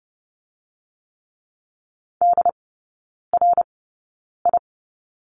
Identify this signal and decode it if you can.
Morse code, heard as a keyed tone, about 30 wpm, spaced as for 9 wpm, 700 Hz. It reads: DRI
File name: Lesson 16e - Letter c - Three Characters 30wpm